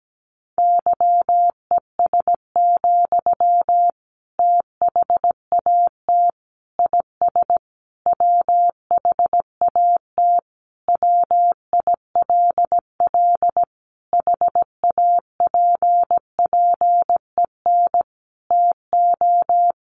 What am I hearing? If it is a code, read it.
Morse code, heard as a keyed tone, about 17 wpm, 700 Hz. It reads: YES, THAT IS WHAT WILL HAPPEN TO
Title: Sentences from Top 500 Words 17wpm